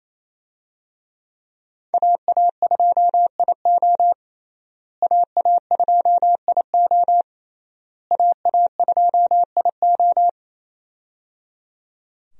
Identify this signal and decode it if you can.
Morse code, heard as a keyed tone, about 28 wpm, 700 Hz. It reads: AA2IO AA2IO AA2IO
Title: USA Call Signs - 5 Characters - Repeated 3x - 28wpm